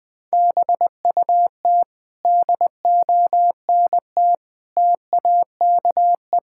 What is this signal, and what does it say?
Morse code, heard as a keyed tone, about 20 wpm, 700 Hz. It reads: BUT DONT TAKE
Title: Sentences with 7-Word Phrases - Spoken Beginning - 20wpm